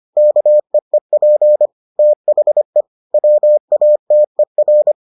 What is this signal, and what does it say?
Morse code, heard as a keyed tone, about 25 wpm, 600 Hz. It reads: KEEP THE WATER